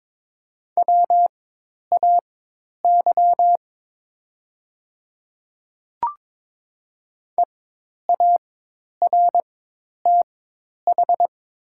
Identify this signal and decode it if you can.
Morse code, heard as a keyed tone, about 22 wpm, spaced as for 10 wpm, 700 Hz. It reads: WAY EARTH
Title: 4x Character Spacing - Top 200 Words - 22wpm